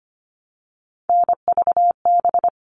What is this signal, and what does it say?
Morse code, heard as a keyed tone, about 25 wpm, 700 Hz. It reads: NVB
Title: All Cut Numbers - 3-Digit 25wpm